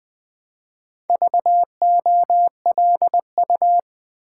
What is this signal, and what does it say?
Morse code, heard as a keyed tone, about 20 wpm, 700 Hz. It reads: VOLU
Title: Build A Word - 2 Letters At A Time - 6-Letter Words - 20wpm